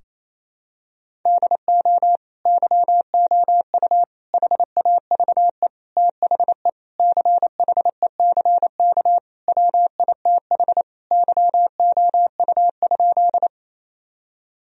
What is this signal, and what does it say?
Morse code, heard as a keyed tone, about 28 wpm, 700 Hz. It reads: DO YOU HAVE THE CHECK WITH YOU?
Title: Sentences from Top 1000 Words - Review Part 3 - 28wpm